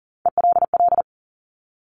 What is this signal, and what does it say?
Morse code, heard as a keyed tone, about 39 wpm, 700 Hz. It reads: ELL
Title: Top Three Letter Combinations in Words 40wpm